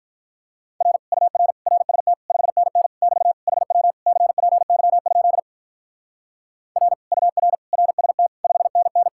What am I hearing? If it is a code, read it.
Morse code, heard as a keyed tone, about 43 wpm, 700 Hz. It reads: R UR RST 5NN <BT> HW CPY? R UR RST 5NN
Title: Speed-Racing Common QSO Phrases - 35wpm